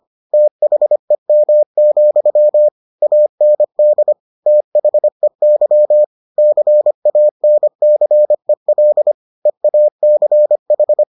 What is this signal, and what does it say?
Morse code, heard as a keyed tone, about 25 wpm, 600 Hz. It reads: THEM, AND THEY CANCEL EACH